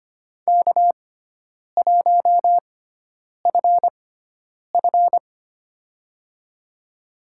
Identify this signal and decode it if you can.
Morse code, heard as a keyed tone, about 25 wpm, spaced as for 9 wpm, 700 Hz. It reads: K1FF
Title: USA Call Signs - 4 Characters - 6x Character Spacing - 25wpm